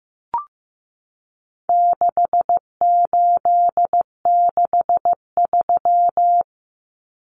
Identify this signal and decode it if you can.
Morse code, heard as a keyed tone, about 15 wpm, 700 Hz. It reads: 6863
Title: Common Cut Numbers - 4-Digit 15wpm